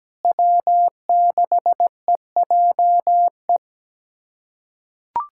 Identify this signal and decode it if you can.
Morse code, heard as a keyed tone, about 17 wpm, 700 Hz. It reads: W6EJE
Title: USA Callsign DE Callsign 17wpm